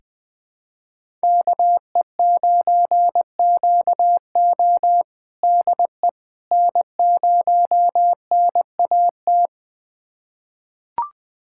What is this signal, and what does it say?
Morse code, heard as a keyed tone, about 20 wpm, 700 Hz. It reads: KE9QO DE N0NAT
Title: USA Callsign DE Callsign - 5 Characters - 20wpm